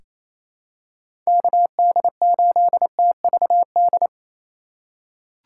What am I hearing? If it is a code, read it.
Morse code, heard as a keyed tone, about 28 wpm, 700 Hz. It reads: KD8TVD
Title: USA Call Signs - 6 Characters - 4x Character Spacing - 28wpm